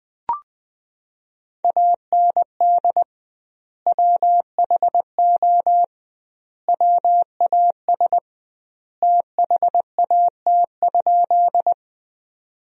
Morse code, heard as a keyed tone, about 20 wpm, 700 Hz. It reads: AND WHO WAS THAT?
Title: 2x Word-Spacing - Sentences from Top 100 Words - 20wpm